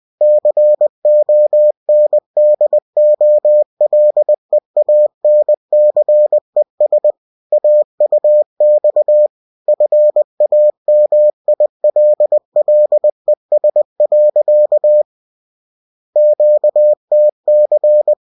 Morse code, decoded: CONDOLEANCES AUX FAMILLES. QTC